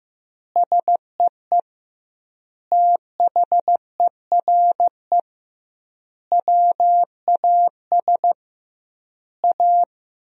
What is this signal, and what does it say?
Morse code, heard as a keyed tone, about 15 wpm, 700 Hz. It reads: SEE THERE WAS A
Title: 2x Word-Spacing - Sentences from Top 500 Words - 15wpm